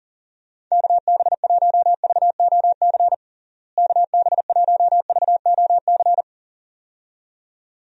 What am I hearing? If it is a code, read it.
Morse code, heard as a keyed tone, about 40 wpm, 700 Hz. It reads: KB1VOC KB1VOC
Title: USA Call Signs - 6 Characters - Repeated 2x - 40wpm